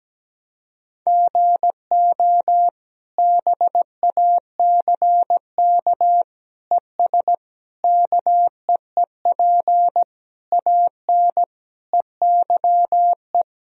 Morse code, heard as a keyed tone, about 17 wpm, 700 Hz. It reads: GO BACK ES KEEP AN EYE
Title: Sentences from Top 300 Words and Abbreviations 17wpm